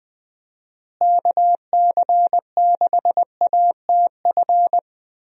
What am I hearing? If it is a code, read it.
Morse code, heard as a keyed tone, about 20 wpm, 700 Hz. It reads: KC6ATF
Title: USA Call Signs - 6 Characters - Repeated 3x - 20wpm